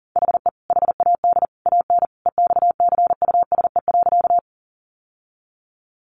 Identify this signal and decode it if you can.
Morse code, heard as a keyed tone, about 40 wpm, 700 Hz. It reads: HE HAD AN EXCUSE.
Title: Sentences from Top 800 Words 40wpm